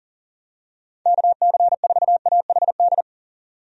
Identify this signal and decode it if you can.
Morse code, heard as a keyed tone, about 38 wpm, 700 Hz. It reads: KC4AHD
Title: USA Call Signs - 6 Characters - 40wpm